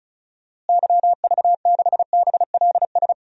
Morse code, heard as a keyed tone, about 36 wpm, 700 Hz. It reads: YV6BLS
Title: DXCC-100 Call Signs - 6 Characters - Repeated 2x - 35wpm